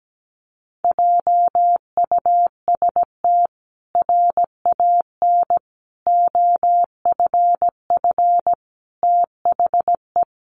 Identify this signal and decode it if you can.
Morse code, heard as a keyed tone, about 17 wpm, 700 Hz. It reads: JUST RAN OFF THE
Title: A Sound Of Thunder 17wpm